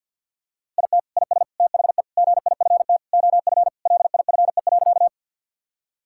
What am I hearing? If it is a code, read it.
Morse code, heard as a keyed tone, about 50 wpm, 700 Hz. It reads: IT IS THE GIFT OF LIFE.
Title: Sentences from Top 1000 Words - Review Part 6 - 50wpm